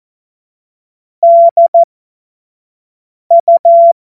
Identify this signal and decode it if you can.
Morse code, heard as a keyed tone, about 14 wpm, spaced as for 5 wpm, 680 Hz. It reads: DU
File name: Morsekurs / Witze / Witz 12. April 2022 25 F70